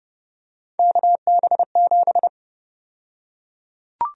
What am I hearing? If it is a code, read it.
Morse code, heard as a keyed tone, about 31 wpm, 700 Hz. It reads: KB7
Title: USA Call Sign Prefixes - Repeated 2x - 30wpm